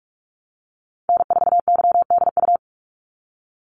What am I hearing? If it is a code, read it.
Morse code, heard as a keyed tone, about 44 wpm, 700 Hz. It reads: N4YDU
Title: Call Signs Repeated 3x - 45wpm